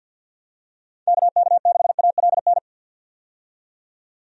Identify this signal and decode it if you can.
Morse code, heard as a keyed tone, about 53 wpm, 700 Hz. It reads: KK6ALN
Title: USA Call Signs - 6 Characters - 50wpm